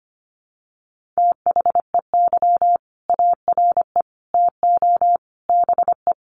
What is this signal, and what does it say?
Morse code, heard as a keyed tone, about 25 wpm, 700 Hz. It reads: THEY ARE TO BE
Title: Sentences from Top 1400 Words 25wpm